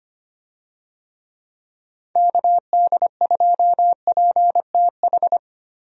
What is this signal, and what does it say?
Morse code, heard as a keyed tone, about 25 wpm, 700 Hz. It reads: KD2PTH
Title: USA Call Signs - 6 Characters - Repeated 3x - 25wpm